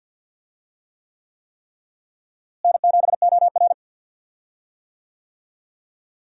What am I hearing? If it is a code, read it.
Morse code, heard as a keyed tone, about 52 wpm, 700 Hz. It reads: N7OR
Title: USA Call Signs - 4 Characters - Repeated 3x - 50wpm